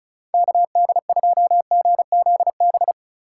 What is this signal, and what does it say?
Morse code, heard as a keyed tone, about 34 wpm, 700 Hz. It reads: KD2GZB
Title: USA Call Signs - 6 Characters - 3x Character Spacing - 35wpm